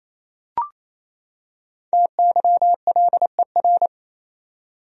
Morse code, heard as a keyed tone, about 28 wpm, 700 Hz. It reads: TYLER
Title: USA Common Names - 28wpm